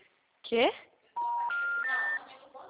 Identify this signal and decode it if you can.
Special information tone, with speech.